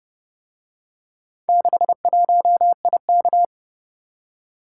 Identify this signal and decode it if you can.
Morse code, heard as a keyed tone, about 30 wpm, 700 Hz. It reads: 61IK